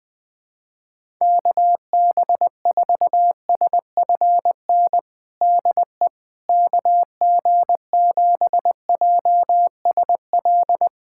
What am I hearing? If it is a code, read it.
Morse code, heard as a keyed tone, about 21 wpm, 700 Hz. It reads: KB4SFN DE KG7JSL
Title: USA Callsign DE Callsign - 6 Characters - 20wpm